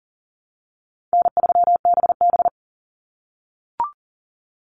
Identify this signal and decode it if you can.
Morse code, heard as a keyed tone, about 41 wpm, 700 Hz. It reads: N3BB
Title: Easy Call Signs - 40wpm